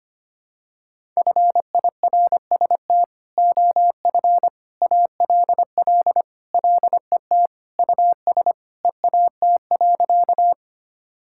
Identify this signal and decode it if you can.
Morse code, heard as a keyed tone, about 25 wpm, 700 Hz. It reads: FIRST OF ALL LET US EAT.
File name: Sentences with 3-Word Phrases - 25wpm